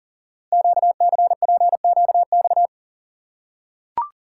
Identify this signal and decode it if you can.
Morse code, heard as a keyed tone, about 43 wpm, 700 Hz. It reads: QCPQX
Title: Code Groups - 5 Letters - 40wpm